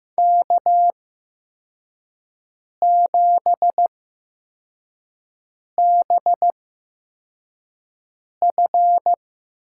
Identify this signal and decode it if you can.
Morse code, heard as a keyed tone, about 15 wpm, spaced as for 4 wpm, 700 Hz. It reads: K7BF
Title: USA Call Signs - 5 Characters - 8x Character Spacing - 15wpm